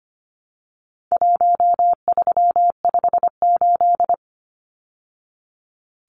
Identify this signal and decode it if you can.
Morse code, heard as a keyed tone, about 25 wpm, 700 Hz. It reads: 1358
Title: Numbers - Contest Style - 4-Digit 25wpm